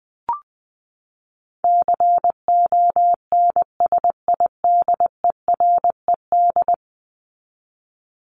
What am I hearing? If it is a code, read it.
Morse code, heard as a keyed tone, about 19 wpm, 700 Hz. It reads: CONSIDERED
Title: ED Suffix - Top 100 Words with Base Words 20wpm